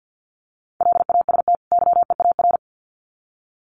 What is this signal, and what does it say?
Morse code, heard as a keyed tone, about 50 wpm, 700 Hz. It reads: LAST YEAR
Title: Sets of 2 Words 50wpm